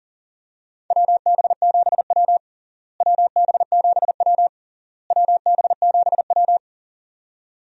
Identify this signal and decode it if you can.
Morse code, heard as a keyed tone, about 40 wpm, 700 Hz. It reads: WB7W WB7W WB7W
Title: USA Call Signs - 4 Characters - Repeated 3x - 40wpm